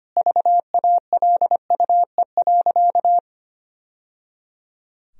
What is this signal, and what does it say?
Morse code, heard as a keyed tone, about 25 wpm, 700 Hz. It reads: VALUE.
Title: Sentences from Top 1000 Words - Review Part 4 - 25wpm